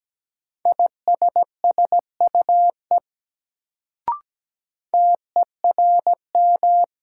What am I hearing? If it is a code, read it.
Morse code, heard as a keyed tone, about 17 wpm, 700 Hz. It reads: ISSUE TERM
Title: Sentences from Top 900 Words - New Word Drill - 17wpm